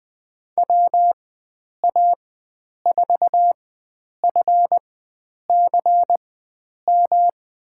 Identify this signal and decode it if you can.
Morse code, heard as a keyed tone, about 20 wpm, spaced as for 9 wpm, 700 Hz. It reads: WA4FCM
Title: USA Call Signs - 6 Characters - 4x Character Spacing - 20wpm